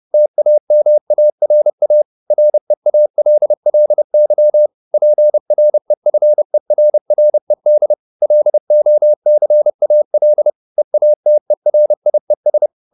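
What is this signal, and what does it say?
Morse code, heard as a keyed tone, about 30 wpm, 600 Hz. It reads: TAMARA REALLY PREFERRED LOCAL EATERIES
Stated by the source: Morse code at 30 wpm, the novel Never by Ken Follett in English, from CW med Gnister.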